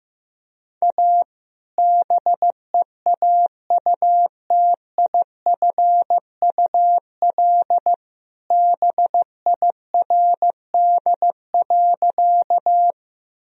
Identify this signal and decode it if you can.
Morse code, heard as a keyed tone, about 16 wpm, 700 Hz. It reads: A BEAUTIFUL BIRD.
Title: Sentences from Top 1000 Words - Review Part 2 - 15wpm